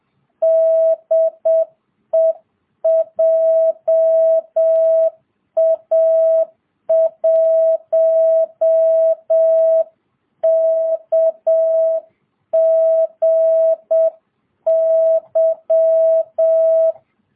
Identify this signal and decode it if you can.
Morse code, heard as a keyed tone, about 7 wpm, 660 Hz. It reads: DE JA1KGY